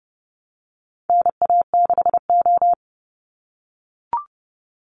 Morse code, heard as a keyed tone, about 29 wpm, 700 Hz. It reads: NA6O
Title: NA SOTA Chaser Callsigns 30wpm